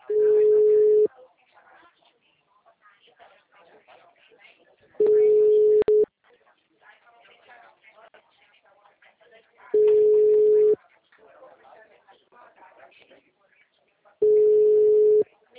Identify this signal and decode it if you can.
Ringback tone (Europe, 425 Hz), with speech.